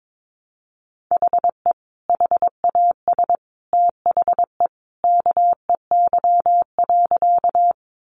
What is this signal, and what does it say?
Morse code, heard as a keyed tone, about 22 wpm, 700 Hz. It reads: HE HAS THE KEY.
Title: Sentences from Top 600 Words 22wpm